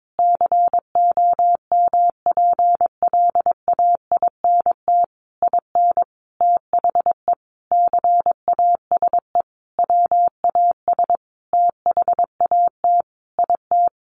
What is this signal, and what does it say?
Morse code, heard as a keyed tone, about 22 wpm, 700 Hz. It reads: COMPLAINT IN THE CASE WAS THAT IT
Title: Sentences from Top 1400 Words 22wpm